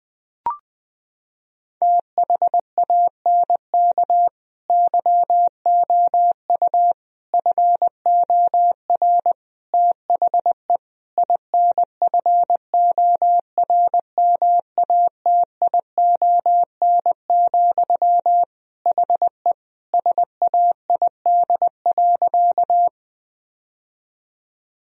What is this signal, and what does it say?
Morse code, heard as a keyed tone, about 21 wpm, 700 Hz. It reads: THANK YOU FOR THE INFORMATION, HE SAID.
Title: Sentences from Top 1000 Words - Review Part 3 - 20wpm